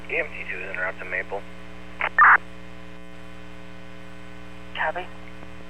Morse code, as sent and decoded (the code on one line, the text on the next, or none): none